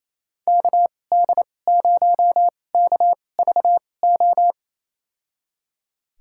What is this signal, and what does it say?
Morse code, heard as a keyed tone, about 28 wpm, spaced as for 20 wpm, 700 Hz. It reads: KD0KVO
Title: USA Call Signs - 2x Character Spacing - 28wpm